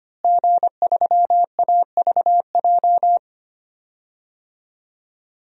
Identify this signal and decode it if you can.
Morse code, heard as a keyed tone, about 25 wpm, 700 Hz. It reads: G3AVJ
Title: DXCC-100 Call Signs - 5 Characters - Repeated 2x - 25wpm